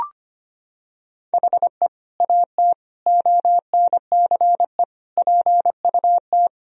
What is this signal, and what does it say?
Morse code, heard as a keyed tone, about 25 wpm, 700 Hz. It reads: HE AT ONCE PUT
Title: Sentences from Top 1000 Words - Review Part 1 - 25wpm